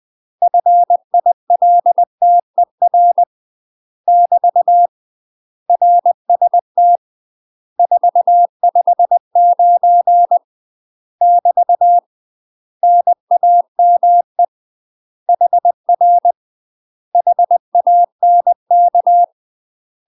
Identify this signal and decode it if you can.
Morse code, heard as a keyed tone, about 20 wpm, 700 Hz. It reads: FILTER <BT> RST 459 <BT> NAME HR HANK
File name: UK MASTER QSOs TRAINING/20WPM